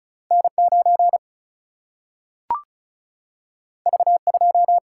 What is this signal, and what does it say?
Morse code, heard as a keyed tone, about 36 wpm, 700 Hz. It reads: N9 V2
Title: Single-Letter Single-Number 35wpm